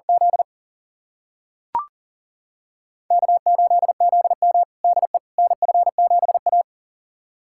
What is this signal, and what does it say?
Morse code, heard as a keyed tone, about 40 wpm, 700 Hz. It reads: Z K8ZM DE NF7A
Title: USA Callsign DE Callsign - 4 Characters - 40wpm